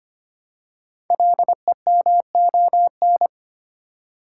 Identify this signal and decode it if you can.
Morse code, heard as a keyed tone, about 25 wpm, 700 Hz. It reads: LEMON
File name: 5 Letter Words - First 3 Letters Given - 25wpm